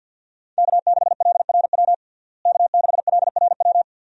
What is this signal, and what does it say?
Morse code, heard as a keyed tone, about 50 wpm, 700 Hz. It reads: K6LRW K6LRW
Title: USA Call Signs - 5 Characters - Repeated 2x - 50wpm